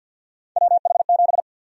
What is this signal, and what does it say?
Morse code, heard as a keyed tone, about 50 wpm, 700 Hz. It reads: WH7
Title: USA Call Sign Prefixes - 50wpm